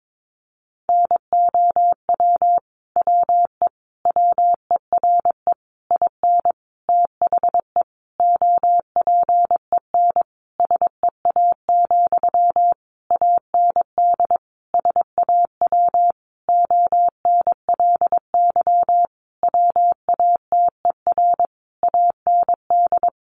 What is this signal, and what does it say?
Morse code, heard as a keyed tone, about 22 wpm, 700 Hz. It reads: NOW WE WERE IN THE OPEN SEA, AND SAW ONLY WATER AND
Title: Sentences from Top 1200 Words 22wpm